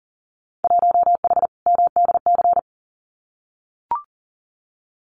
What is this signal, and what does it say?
Morse code, heard as a keyed tone, about 38 wpm, 700 Hz. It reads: 1H MDC
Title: Winter Field Day Exchanges 40wpm